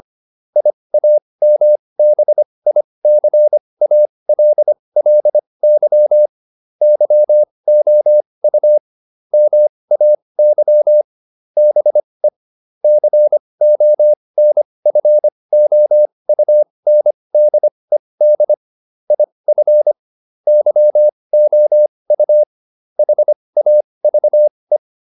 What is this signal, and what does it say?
Morse code, heard as a keyed tone, about 25 wpm, spaced as for 20 wpm, 600 Hz. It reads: IAMBICALLY YOU MAY BE CONFOUNDED IF YOU HAVE